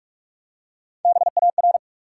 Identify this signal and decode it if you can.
Morse code, heard as a keyed tone, about 44 wpm, 700 Hz. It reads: BAR
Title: Three Letter Words - 45wpm